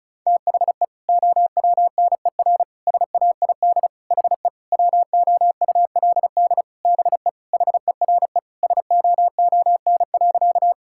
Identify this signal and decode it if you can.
Morse code, heard as a keyed tone, about 34 wpm, 700 Hz. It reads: THE OWNER SAID HE WOULD BE HERE SOON.
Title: Sentences from Top 1600 Words 35wpm